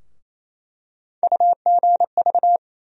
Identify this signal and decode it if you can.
Morse code, heard as a keyed tone, about 28 wpm, 700 Hz. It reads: UGV
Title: USA Call Sign Postfixes - 3 Letters 28wpm